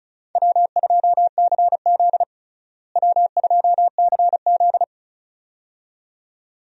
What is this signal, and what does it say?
Morse code, heard as a keyed tone, about 36 wpm, 700 Hz. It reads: W2CZ W2CZ
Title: USA Call Signs - 4 Characters - Repeated 2x - 35wpm